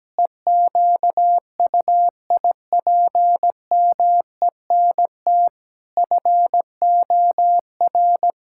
Morse code, heard as a keyed tone, about 17 wpm, 700 Hz. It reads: EQUIPMENT FOR
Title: Sentences from Top 1000 Words - Review Part 5 - 17wpm